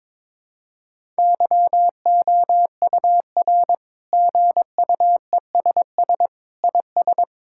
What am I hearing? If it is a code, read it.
Morse code, heard as a keyed tone, about 22 wpm, 700 Hz. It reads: YOUR GUESS IS
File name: practice / 22wpm / Sets of 7 Words - 22wpm